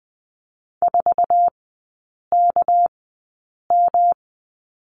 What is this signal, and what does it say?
Morse code, heard as a keyed tone, about 20 wpm, spaced as for 8 wpm, 700 Hz. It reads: 4KM
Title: 198 Lesson 32e - Letter k - Three Characters 20wpm